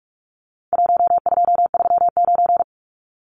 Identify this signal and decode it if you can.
Morse code, heard as a keyed tone, about 44 wpm, 700 Hz. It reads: 1239
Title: Numbers - 4-Digit 45wpm